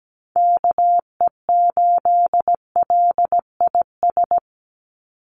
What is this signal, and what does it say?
Morse code, heard as a keyed tone, about 17 wpm, 700 Hz. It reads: KE8LIS
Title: USA Call Signs 17wpm